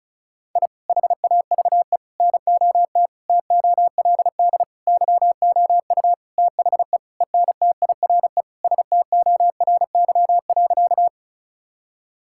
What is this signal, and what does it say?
Morse code, heard as a keyed tone, about 34 wpm, 700 Hz. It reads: I HAVE NOT TOLD YOU THE ENTIRE STORY.
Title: Sentences from Top 1000 Words - Review Part 5 - 35wpm